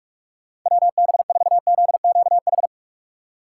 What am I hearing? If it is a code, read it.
Morse code, heard as a keyed tone, about 46 wpm, 700 Hz. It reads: WB4ZQH
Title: USA Call Signs - 3x Character Spacing - 45wpm